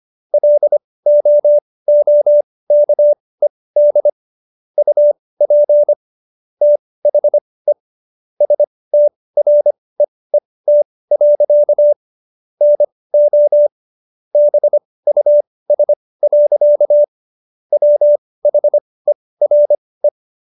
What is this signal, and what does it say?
Morse code, heard as a keyed tone, about 25 wpm, spaced as for 18 wpm, 590 Hz. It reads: LOOKED UP THE STREET. NO BUS. WHERE